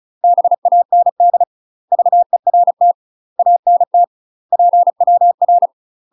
Morse code, heard as a keyed tone, about 34 wpm, 700 Hz. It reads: BAND VERT ANT PWR
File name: UK MASTER QSOs TRAINING / 35WPM / UK MASTER QSOs 0013